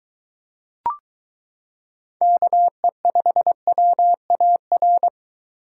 Morse code, heard as a keyed tone, about 23 wpm, 700 Hz. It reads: KE5WAR